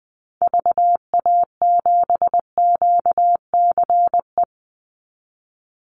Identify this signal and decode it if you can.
Morse code, heard as a keyed tone, about 20 wpm, 700 Hz. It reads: VA7QCE
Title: Call Signs Repeated 3x - 20wpm